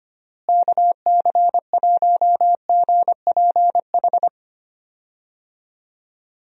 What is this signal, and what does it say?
Morse code, heard as a keyed tone, about 25 wpm, 700 Hz. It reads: KC1GPH
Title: USA Call Signs - 6 Characters - Repeated 3x - 25wpm